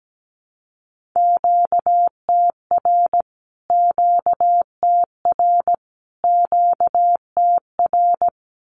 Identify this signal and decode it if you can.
Morse code, heard as a keyed tone, about 17 wpm, 700 Hz. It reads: QTR QTR QTR
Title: Q-Codes Repeated 3x 17wpm